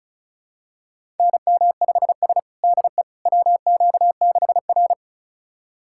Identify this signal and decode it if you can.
Morse code, heard as a keyed tone, about 36 wpm, 700 Hz. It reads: NM5S DE WQ6R
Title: USA Callsign DE Callsign - 4 Characters - 35wpm